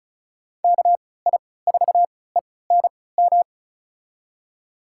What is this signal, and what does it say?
Morse code, heard as a keyed tone, about 36 wpm, spaced as for 20 wpm, 700 Hz. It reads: KI4ENM
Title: USA Call Signs - 6 Characters - 3x Character Spacing - 35wpm